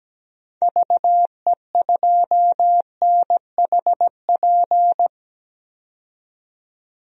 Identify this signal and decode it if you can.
Morse code, heard as a keyed tone, about 17 wpm, 700 Hz. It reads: VE2NHP
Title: DXCC-100 Call Signs - 6 Characters - 17wpm